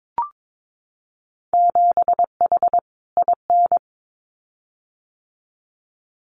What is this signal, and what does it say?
Morse code, heard as a keyed tone, about 22 wpm, 700 Hz. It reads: 7H IN
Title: Winter Field Day Exchanges Encore 22wpm